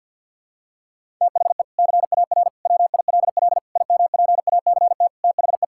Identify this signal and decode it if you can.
Morse code, heard as a keyed tone, about 52 wpm, 700 Hz. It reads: THE CAR WILL IMPACT THE
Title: Sentences from Top 1000 Words - Review Part 6 - 50wpm